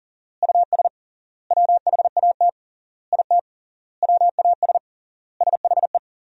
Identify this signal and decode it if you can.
Morse code, heard as a keyed tone, about 40 wpm, 700 Hz. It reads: US WHAT IT WAS SHE
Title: 3x Word-Spacing - Sentences from Top 400 Words - 40wpm